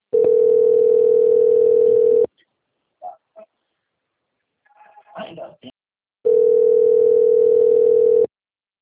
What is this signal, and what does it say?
Ringback tone (North America), 440 + 480 Hz, with speech.
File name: out- - - - - .55833